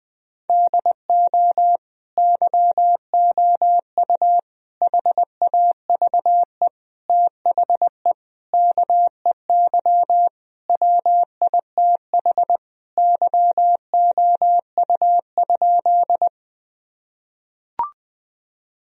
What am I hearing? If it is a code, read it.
Morse code, heard as a keyed tone, about 20 wpm, 700 Hz. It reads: DO YOU HAVE THE KEY WITH YOU?
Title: Sentences from Top 1000 Words - Review Part 4 - 20wpm